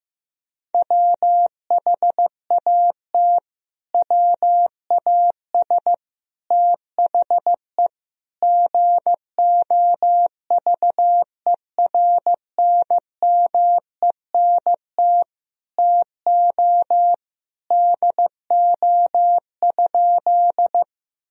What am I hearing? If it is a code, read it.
Morse code, heard as a keyed tone, about 16 wpm, 700 Hz. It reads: WHAT WAS THE GOVERNMENT TO DO?
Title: Sentences from Top 1000 Words - Review Part 3 - 15wpm